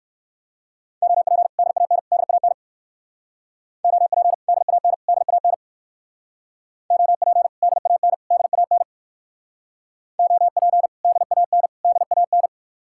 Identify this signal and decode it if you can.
Morse code, heard as a keyed tone, about 57 wpm, 700 Hz. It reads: OP DAN DAN OP DAN DAN OP DAN DAN OP DAN DAN